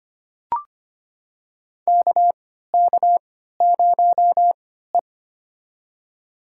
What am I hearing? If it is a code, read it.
Morse code, heard as a keyed tone, about 25 wpm, spaced as for 14 wpm, 700 Hz. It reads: KK0E
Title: USA Call Signs - 4 Characters - 3x Character Spacing - 25wpm